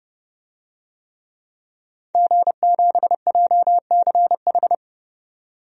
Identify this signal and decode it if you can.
Morse code, heard as a keyed tone, about 32 wpm, 700 Hz. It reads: G7JCH
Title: DXCC-100 Call Signs - 5 Characters - 30wpm